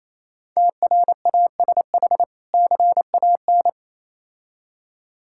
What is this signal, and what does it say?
Morse code, heard as a keyed tone, about 27 wpm, 700 Hz. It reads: TRASH CAN
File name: Easy Words - Speed-Racing with Hints - 22wpm